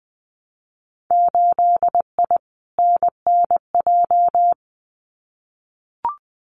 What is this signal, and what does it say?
Morse code, heard as a keyed tone, about 20 wpm, 700 Hz. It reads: 8I NNJ
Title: Winter Field Day Exchanges Encore 20wpm